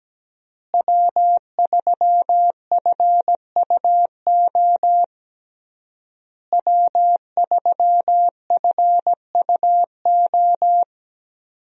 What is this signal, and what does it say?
Morse code, heard as a keyed tone, about 17 wpm, 700 Hz. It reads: W3FUO W3FUO
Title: USA Call Signs - 5 Characters - Repeated 3x - 17wpm